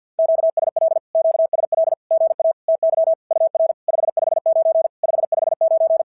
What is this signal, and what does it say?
Morse code, heard as a keyed tone, about 46 wpm, 660 Hz. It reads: QSL QSL GA TY UR 559 559
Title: SOTA QSO Practice 50wpm